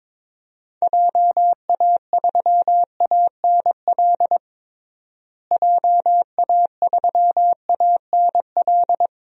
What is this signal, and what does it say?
Morse code, heard as a keyed tone, about 21 wpm, 700 Hz. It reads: JA3ANL JA3ANL
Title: DXCC-100 Call Signs - 6 Characters - Repeated 2x - 22wpm